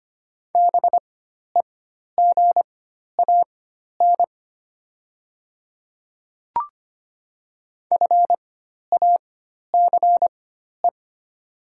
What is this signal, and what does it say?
Morse code, heard as a keyed tone, about 25 wpm, spaced as for 12 wpm, 700 Hz. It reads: BEGAN FACE
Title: 4x Character Spacing - Top 300 Words - 25wpm